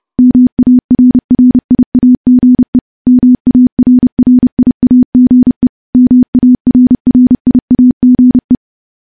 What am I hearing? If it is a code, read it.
Morse code, heard as a keyed tone, about 30 wpm, 250 Hz. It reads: MARRIAGE MARRIAGE MARRIAGE
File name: morse / morse-training-30WPM